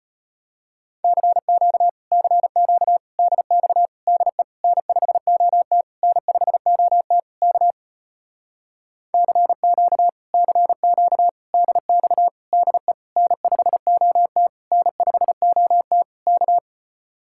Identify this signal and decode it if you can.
Morse code, heard as a keyed tone, about 33 wpm, 700 Hz. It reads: CQ CQ DX DE N5OT N5OT K CQ CQ DX DE N5OT N5OT K